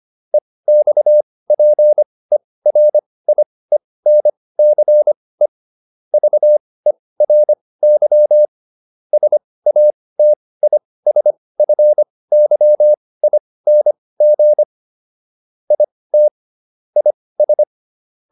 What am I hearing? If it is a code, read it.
Morse code, heard as a keyed tone, about 25 wpm, spaced as for 18 wpm, 600 Hz. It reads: EXPERIENCE VERY SATISFYING IT IS